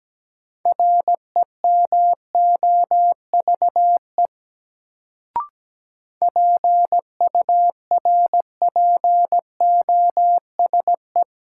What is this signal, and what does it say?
Morse code, heard as a keyed tone, about 17 wpm, 700 Hz. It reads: REMOVE PURPOSE
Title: Sentences from Top 800 Words - New Word Drill - 17wpm